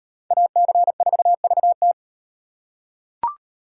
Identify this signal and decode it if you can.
Morse code, heard as a keyed tone, about 38 wpm, 700 Hz. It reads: AC4VT